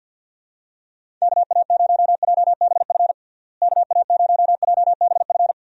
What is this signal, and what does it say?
Morse code, heard as a keyed tone, about 50 wpm, 700 Hz. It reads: KA0JBF KA0JBF
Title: USA Call Signs - 6 Characters - Repeated 3x - 50wpm